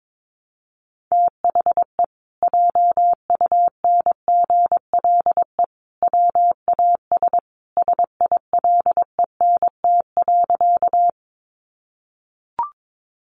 Morse code, heard as a keyed tone, about 22 wpm, 700 Hz. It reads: THE JUNGLE WAS SILENT.
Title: A Sound Of Thunder 22wpm